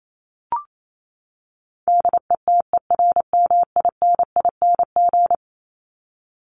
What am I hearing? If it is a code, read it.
Morse code, heard as a keyed tone, about 28 wpm, 700 Hz. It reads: DETERMINING
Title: ING Suffix - Top 100 Words with Base Words 28wpm